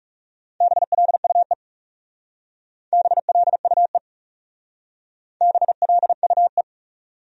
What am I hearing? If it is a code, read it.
Morse code, heard as a keyed tone, about 36 wpm, 700 Hz. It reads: BLUE BLUE BLUE
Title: Speed-Racing Top 500 Words 30wpm